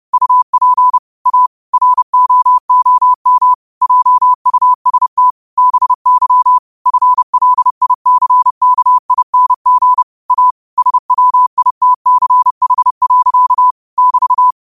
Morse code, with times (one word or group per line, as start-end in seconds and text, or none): 0.1-1.0 UP
1.3-1.5 A
1.7-3.5 ROOM
3.8-5.3 JUST
5.6-6.6 BY
6.9-10.0 FLICKING
10.3-10.5 A
10.8-13.7 SWITCH.
14.0-14.5 <BT>